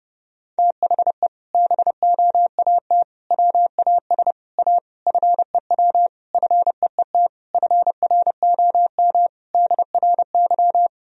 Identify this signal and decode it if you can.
Morse code, heard as a keyed tone, about 30 wpm, 700 Hz. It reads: THE BOAT WAS A FEW FEET FROM DRY
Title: Sentences from Top 1000 Words - Review Part 3 - 30wpm